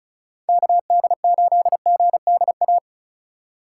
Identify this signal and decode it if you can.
Morse code, heard as a keyed tone, about 43 wpm, 700 Hz. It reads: KD8GDA
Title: Speed Racing - USA Call Signs 35wpm